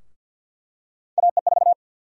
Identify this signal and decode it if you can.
Morse code, heard as a keyed tone, about 51 wpm, 700 Hz. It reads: AE4
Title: USA Call Sign Prefixes - 50wpm